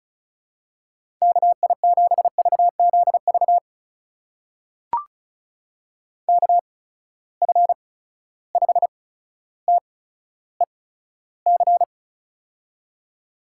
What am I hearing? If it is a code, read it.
Morse code, heard as a keyed tone, about 36 wpm, spaced as for 15 wpm, 700 Hz. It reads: KI7VZV KF5TEC
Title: USA Call Signs - 6 Characters - 8x Character Spacing - 35wpm